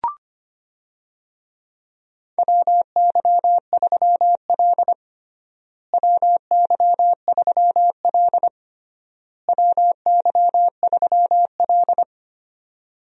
Morse code, heard as a keyed tone, about 25 wpm, 700 Hz. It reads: WY3L WY3L WY3L
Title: USA Call Signs - 4 Characters - Repeated 3x - 25wpm